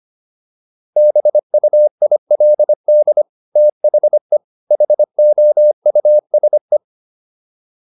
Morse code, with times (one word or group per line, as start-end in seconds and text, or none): 1.0-3.2 BUILD
3.6-4.4 THE
4.7-6.8 HOUSE